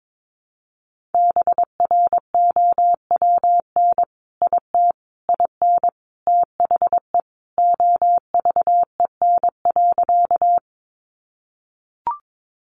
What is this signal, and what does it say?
Morse code, heard as a keyed tone, about 22 wpm, 700 Hz. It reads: BROWN IT IN THE OVEN.
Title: Sentences from Top 1200 Words 22wpm